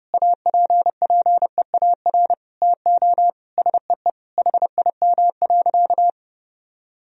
Morse code, heard as a keyed tone, about 30 wpm, 700 Hz. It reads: APPEAR TO SEE HIM.
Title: Sentences from Top 1000 Words - Review Part 2 - 30wpm